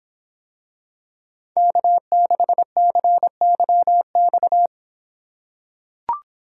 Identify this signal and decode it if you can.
Morse code, heard as a keyed tone, about 25 wpm, 700 Hz. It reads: K6CYX